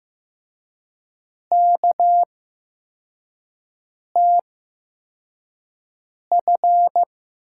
Morse code, heard as a keyed tone, about 15 wpm, spaced as for 4 wpm, 700 Hz. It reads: KTF
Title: USA Call Signs - 8x Character Spacing - 15wpm